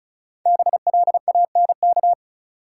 Morse code, heard as a keyed tone, about 36 wpm, 700 Hz. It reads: BLANK
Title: Sentences from Top 1500 Words - New Word Drill - 35wpm